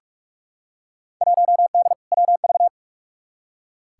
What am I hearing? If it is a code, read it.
Morse code, heard as a keyed tone, about 46 wpm, 700 Hz. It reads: 1D WV
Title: Field Day Exchanges - 45wpm